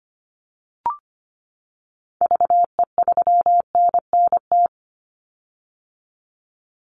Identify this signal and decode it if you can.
Morse code, heard as a keyed tone, about 25 wpm, 700 Hz. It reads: VE3NNT
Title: Call Signs - 25wpm